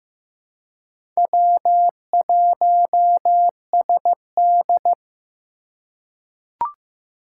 Morse code, heard as a keyed tone, about 15 wpm, 700 Hz. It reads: W1SD
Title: USA Call Signs - 4 Characters - 3x Character Spacing - 15wpm